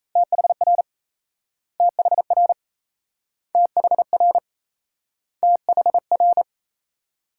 Morse code, heard as a keyed tone, about 36 wpm, 700 Hz. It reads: THR THR THR THR